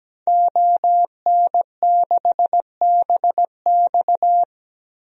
Morse code, heard as a keyed tone, about 17 wpm, 700 Hz. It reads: ON6BX
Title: DXCC-100 Call Signs - 5 Characters - 17wpm